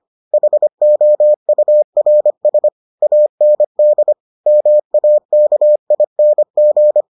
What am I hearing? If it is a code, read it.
Morse code, heard as a keyed tone, about 25 wpm, 600 Hz. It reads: HOURS AND MAKING